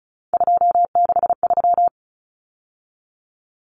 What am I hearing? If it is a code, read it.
Morse code, heard as a keyed tone, about 34 wpm, 700 Hz. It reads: 263
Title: Numbers - 3-Digit 35wpm